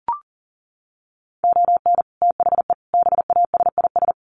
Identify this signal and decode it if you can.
Morse code, heard as a keyed tone, about 40 wpm, 700 Hz. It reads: ON THE BASIS
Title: Sets of 5 Words 40wpm